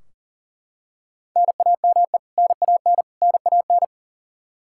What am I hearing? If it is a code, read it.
Morse code, heard as a keyed tone, about 40 wpm, 700 Hz. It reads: NAME NAN NAN
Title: K1USN SST QSO Practice 40wpm